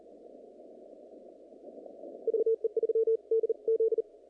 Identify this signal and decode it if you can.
Morse code, heard as a keyed tone, about 34 wpm, 440 Hz. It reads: VE3DZ
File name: contests / naqp-cw-k5tr-op-k5pi- -2200